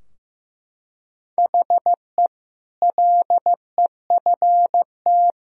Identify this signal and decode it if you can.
Morse code, heard as a keyed tone, about 15 wpm, 700 Hz. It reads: HE LEFT
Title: Sentences from Top 1000 Words - Review Part 3 - 15wpm